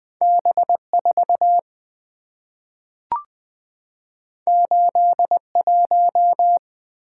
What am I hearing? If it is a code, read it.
Morse code, heard as a keyed tone, about 20 wpm, 700 Hz. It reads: B4 81
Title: Two Letter-Number Combinations - 20wpm